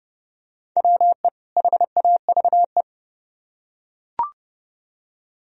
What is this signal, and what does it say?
Morse code, heard as a keyed tone, about 28 wpm, 700 Hz. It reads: WE HAVE